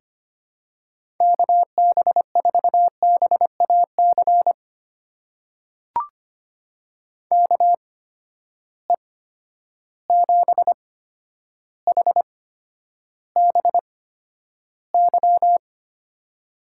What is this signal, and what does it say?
Morse code, heard as a keyed tone, about 25 wpm, spaced as for 11 wpm, 700 Hz. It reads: KB4BAC KE7HBY
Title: USA Call Signs - 6 Characters - 8x Character Spacing - 25wpm